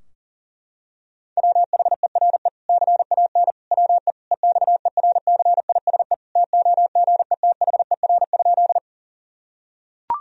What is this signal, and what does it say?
Morse code, heard as a keyed tone, about 40 wpm, 700 Hz. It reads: WHERE CAN WE EXERCISE TOGETHER?
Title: Sentences from Top 1000 Words - Review Part 4 - 40wpm